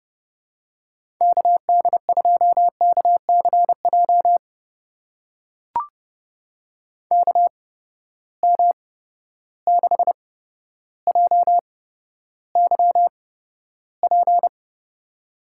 Morse code, heard as a keyed tone, about 30 wpm, spaced as for 13 wpm, 700 Hz. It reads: KD2KCJ KM6JYP
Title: USA Call Signs - 6 Characters - 8x Character Spacing - 30wpm